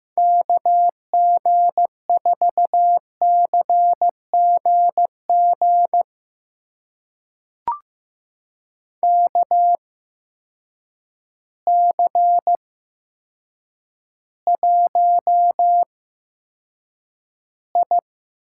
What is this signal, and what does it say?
Morse code, heard as a keyed tone, about 15 wpm, spaced as for 8 wpm, 700 Hz. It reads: KG4CGG KC1I
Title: USA Call Signs - 6 Characters - 8x Character Spacing - 15wpm